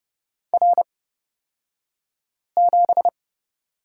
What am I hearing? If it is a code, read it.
Morse code, heard as a keyed tone, about 30 wpm, spaced as for 5 wpm, 690 Hz. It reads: R7
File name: Lesson 36d - BT - Two Characters 30wpm